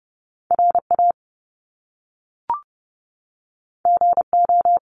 Morse code, heard as a keyed tone, about 30 wpm, 700 Hz. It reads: RA GO
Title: Top Two Letter Combinations in Words 30wpm